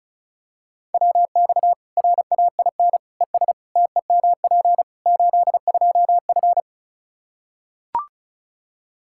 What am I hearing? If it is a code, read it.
Morse code, heard as a keyed tone, about 34 wpm, 700 Hz. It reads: WX RAIN ES TEMP 82F
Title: Practice QSO - Exchange 2 - Info - 35wpm